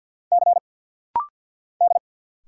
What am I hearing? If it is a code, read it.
Morse code, heard as a keyed tone, about 49 wpm, 700 Hz. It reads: C D
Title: Single Letter-Number Alt - Rapid-Fire 50wpm